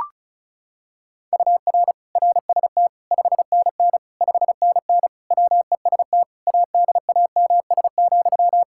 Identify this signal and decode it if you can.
Morse code, heard as a keyed tone, about 36 wpm, 700 Hz. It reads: UR RST 5NN 5NN WEST ADAMS,
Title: K1USN SST QSO Practice 35wpm